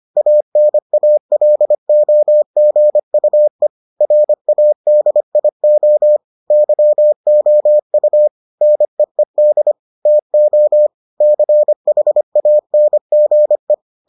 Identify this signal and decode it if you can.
Morse code, heard as a keyed tone, about 25 wpm, 600 Hz. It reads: ANALOGUE RADIO YOU NEED TO CHANGE